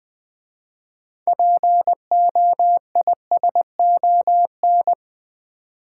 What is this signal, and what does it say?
Morse code, heard as a keyed tone, about 20 wpm, 700 Hz. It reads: POISON
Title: 6 Letter Words - First 3 Letters Given - 20wpm